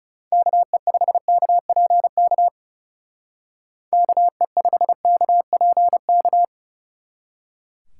Morse code, heard as a keyed tone, about 37 wpm, 700 Hz. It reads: KE5KPK KE5KPK
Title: Speed Racing - USA Call Signs 30wpm